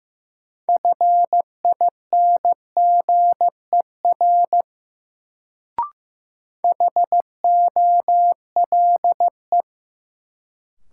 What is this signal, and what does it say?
Morse code, heard as a keyed tone, about 15 wpm, 700 Hz. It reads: FINGER HOLE
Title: Sentences from Top 1600 Words - New Word Drill - 15wpm